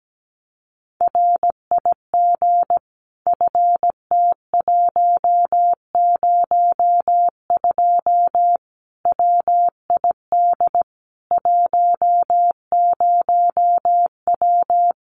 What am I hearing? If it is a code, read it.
Morse code, heard as a keyed tone, about 17 wpm, 700 Hz. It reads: RIG FT102 WID 10W
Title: Common QSO Phrases 17wpm